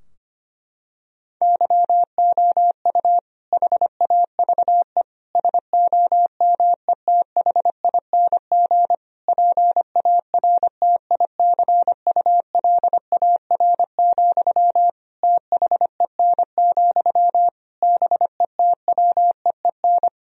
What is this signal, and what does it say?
Morse code, heard as a keyed tone, about 25 wpm, 700 Hz. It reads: YOU HAVE SOMETHING PARTICULAR, THEN, BETWEEN
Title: Sentences from Top 1000 Words - Review Part 4 - 25wpm